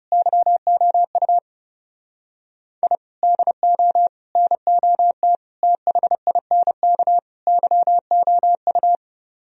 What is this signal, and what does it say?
Morse code, heard as a keyed tone, about 36 wpm, 700 Hz. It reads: YOU I DO NOT THINK YOU
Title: Speed-Racing - Sets of 5 Words 30wpm